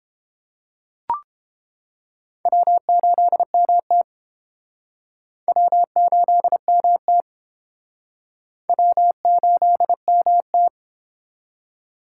Morse code, heard as a keyed tone, about 29 wpm, 700 Hz. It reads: W8MT W8MT W8MT